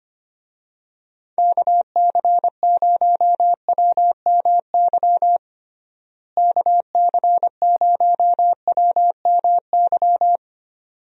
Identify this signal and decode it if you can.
Morse code, heard as a keyed tone, about 25 wpm, 700 Hz. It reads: KC0WMY KC0WMY
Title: USA Call Signs - 6 Characters - Repeated 2x - 25wpm